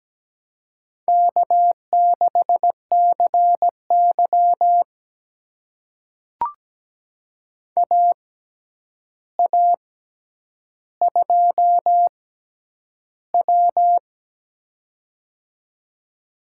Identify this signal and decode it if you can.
Morse code, heard as a keyed tone, about 17 wpm, spaced as for 9 wpm, 700 Hz. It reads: K6CY AA2W
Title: USA Call Signs - 4 Characters - 6x Character Spacing - 17wpm